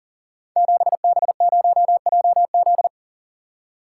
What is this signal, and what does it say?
Morse code, heard as a keyed tone, about 38 wpm, 700 Hz. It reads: 7B0JZ
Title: Code Groups - 5 Letter:Numbers - 40wpm